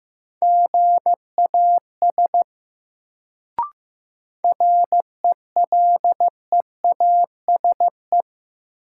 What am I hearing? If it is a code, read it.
Morse code, heard as a keyed tone, about 15 wpm, 700 Hz. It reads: GAS RELEASE
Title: Sentences from Top 1100 Words - New Word Drill Alt - 15wpm